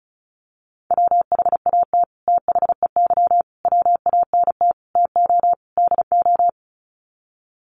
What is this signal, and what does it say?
Morse code, heard as a keyed tone, about 34 wpm, 700 Hz. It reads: WHAT THEY WANT TO DO
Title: Sets of 5 Words Encore2 35wpm